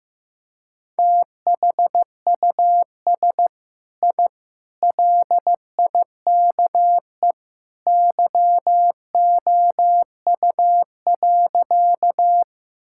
Morse code, heard as a keyed tone, about 15 wpm, 700 Hz. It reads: THUS I LIKE YOU.
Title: Sentences from Top 1000 Words - Review Part 5 - 15wpm